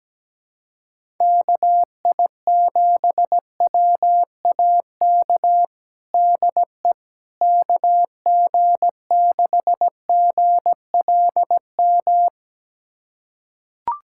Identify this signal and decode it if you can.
Morse code, heard as a keyed tone, about 17 wpm, 700 Hz. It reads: KI7WAK DE KG6GLM
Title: USA Callsign DE Callsign - 6 Characters - 17wpm